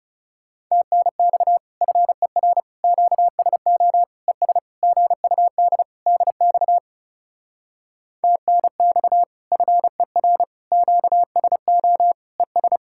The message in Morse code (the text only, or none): TNX FER QSO ES GUD DX TNX FER QSO ES